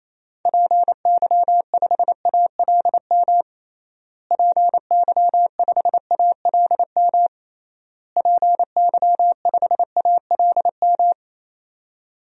Morse code, heard as a keyed tone, about 28 wpm, 700 Hz. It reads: PY5ALM PY5ALM PY5ALM
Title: DXCC-100 Call Signs - 6 Characters - Repeated 3x - 28wpm